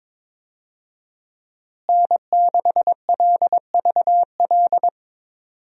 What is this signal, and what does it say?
Morse code, heard as a keyed tone, about 22 wpm, 700 Hz. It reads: N6LVL
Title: USA Call Signs - 5 Characters - Repeated 2x - 22wpm